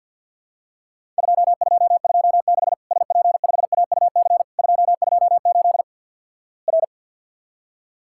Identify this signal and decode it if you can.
Morse code, heard as a keyed tone, about 50 wpm, 700 Hz. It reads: 2226 SP5AUC 228 R
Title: Worked All Europe DX Contest - QTC Exchange with Repeats - 50wpm